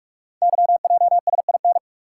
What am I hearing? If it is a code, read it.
Morse code, heard as a keyed tone, about 44 wpm, 690 Hz. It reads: YJSIN
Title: Code Groups - 5 Letters - 45wpm